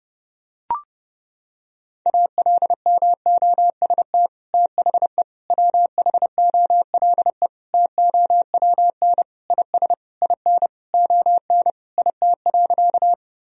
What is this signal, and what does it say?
Morse code, heard as a keyed tone, about 30 wpm, 700 Hz. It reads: ALMOST THE WHOLE TOWN IS IN ON IT.
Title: Sentences from Top 1000 Words - Review Part 4 - 30wpm